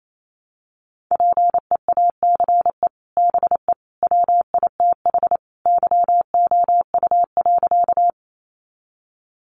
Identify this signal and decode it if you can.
Morse code, heard as a keyed tone, about 28 wpm, 700 Hz. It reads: PEACE BE WITH YOU.
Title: Sentences from Top 1500 Words 28wpm